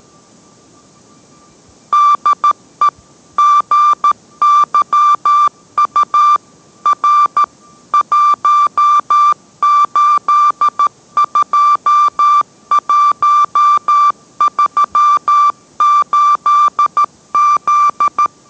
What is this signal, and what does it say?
Morse code, heard as a keyed tone, about 13 wpm, 1180 Hz. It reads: DE GYU R 182138Z